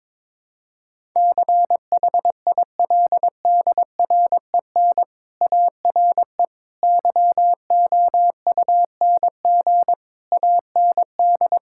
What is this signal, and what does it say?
Morse code, heard as a keyed tone, about 22 wpm, 700 Hz. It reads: CHILDREN ARE YOUNG AND
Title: Sentences from Top 1000 Words - Review Part 6 - 22wpm